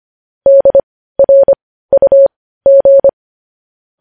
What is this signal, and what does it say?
Morse code, heard as a keyed tone, about 25 wpm, spaced as for 15 wpm, 550 Hz. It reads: DRUG
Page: news-headlines-in-morse-code-at-15-wpm